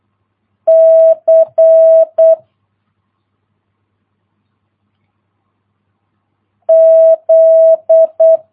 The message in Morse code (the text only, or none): C Z